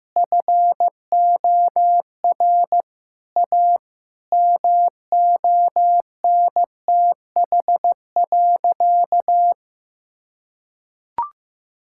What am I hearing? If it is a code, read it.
Morse code, heard as a keyed tone, about 15 wpm, 700 Hz. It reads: FOR A MONTH.
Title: Sentences from Top 1000 Words - Review Part 5 - 15wpm